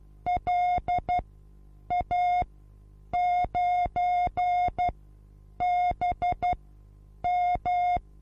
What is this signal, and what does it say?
Morse code, heard as a keyed tone, about 12 wpm, spaced as for 8 wpm, 720 Hz. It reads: LA9BM
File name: repos / Morsekurs / DJ4UF Morsen23